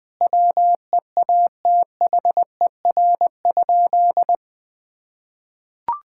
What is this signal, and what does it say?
Morse code, heard as a keyed tone, about 21 wpm, 700 Hz. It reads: WEATHER?
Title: Sentences from Top 1000 Words - Review Part 5 - 20wpm